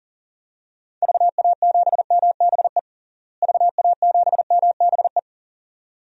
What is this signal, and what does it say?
Morse code, heard as a keyed tone, about 40 wpm, 700 Hz. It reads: VA7MBE VA7MBE
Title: DXCC-100 Call Signs - 6 Characters - Repeated 2x - 40wpm